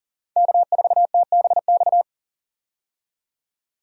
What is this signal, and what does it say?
Morse code, heard as a keyed tone, about 40 wpm, 700 Hz. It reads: K4TBX
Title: USA Call Signs - 5 Characters - 40wpm